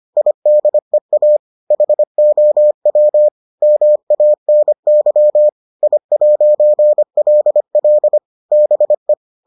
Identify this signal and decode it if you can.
Morse code, heard as a keyed tone, about 25 wpm, 600 Hz. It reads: IDEA HOW MANY I'LL BE